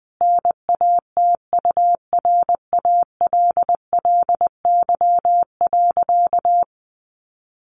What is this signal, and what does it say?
Morse code, heard as a keyed tone, about 20 wpm, 700 Hz. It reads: NATURALLY.
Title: Sentences from Top 1300 Words 20wpm